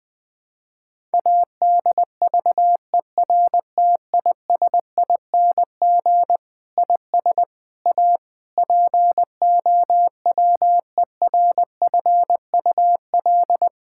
Morse code, heard as a keyed tone, about 20 wpm, 700 Hz. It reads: ADVERTISING IS A POWERFUL
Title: Sentences from Top 1000 Words - Review Part 6 - 20wpm